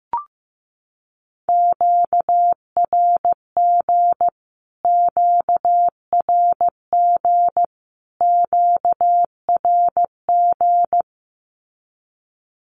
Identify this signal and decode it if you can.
Morse code, heard as a keyed tone, about 15 wpm, 700 Hz. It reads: QRG QRG QRG
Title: Q-Codes Repeated 3x 15wpm